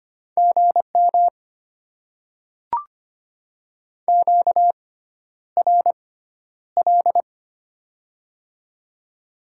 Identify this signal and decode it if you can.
Morse code, heard as a keyed tone, about 25 wpm, spaced as for 11 wpm, 700 Hz. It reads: GM QRL
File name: Top 20 QSO Elements - 6x Character Spacing - 25wpm